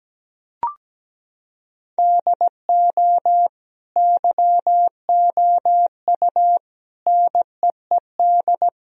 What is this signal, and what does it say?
Morse code, heard as a keyed tone, about 17 wpm, 700 Hz. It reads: DO YOU NEED
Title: Sentences from Top 1000 Words - Review Part 4 - 17wpm